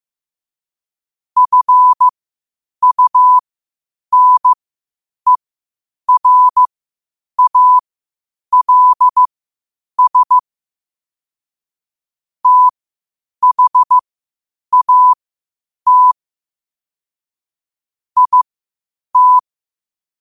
Morse code, heard as a keyed tone, about 15 wpm, spaced as for 8 wpm, 1000 Hz. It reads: FUNERALS THAT IT
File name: qotdfiles / 8wpm